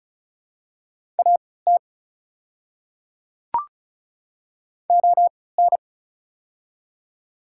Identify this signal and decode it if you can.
Morse code, heard as a keyed tone, about 36 wpm, spaced as for 20 wpm, 700 Hz. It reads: AT ON
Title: 3x Character Spacing - Top 100 Words - 35wpm